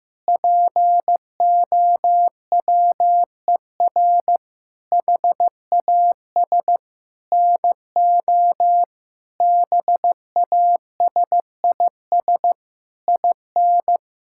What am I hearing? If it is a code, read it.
Morse code, heard as a keyed tone, about 15 wpm, 700 Hz. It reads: POWER HAS NO BASIS IN
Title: Sentences from Top 1000 Words - Review Part 5 - 15wpm